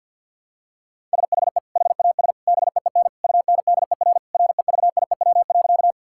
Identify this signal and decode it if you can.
Morse code, heard as a keyed tone, about 46 wpm, 700 Hz. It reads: SHE HAS BEEN UNDER REVIEW.
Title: Sentences from Top 1000 Words - Review Part 6 - 50wpm